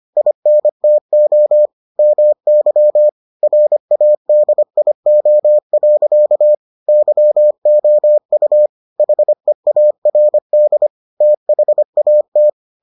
Morse code, heard as a keyed tone, about 25 wpm, 600 Hz. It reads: INTO MY RADIO. YOU HEARD THAT